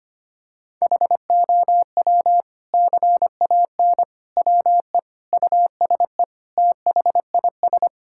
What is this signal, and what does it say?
Morse code, heard as a keyed tone, about 25 wpm, 700 Hz. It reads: HOW CAN WE USE THIS
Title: Sentences from Top 1000 Words - Review Part 5 - 25wpm